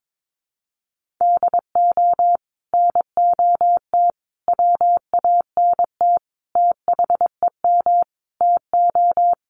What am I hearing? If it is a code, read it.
Morse code, heard as a keyed tone, about 22 wpm, 700 Hz. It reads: DO NOT WANT THEM TO
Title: Sets of 5 Words Encore 22wpm